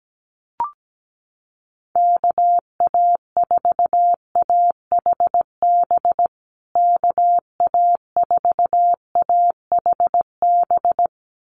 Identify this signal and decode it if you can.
Morse code, heard as a keyed tone, about 17 wpm, 700 Hz. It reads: KA4AHB KA4AHB
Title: USA Call Signs Repeated 2x 17wpm